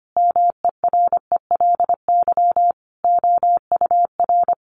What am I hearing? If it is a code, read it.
Morse code, heard as a keyed tone, about 25 wpm, 700 Hz. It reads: MERELY OUR
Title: Sentences from Top 1400 Words 25wpm